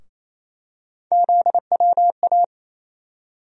Morse code, heard as a keyed tone, about 27 wpm, 700 Hz. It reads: ZWA
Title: USA Call Sign Postfixes - 3 Letters 28wpm